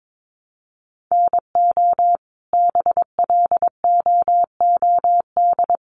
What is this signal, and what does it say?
Morse code, heard as a keyed tone, about 22 wpm, 700 Hz. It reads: NO BLOOD
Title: Sentences from Top 1200 Words 22wpm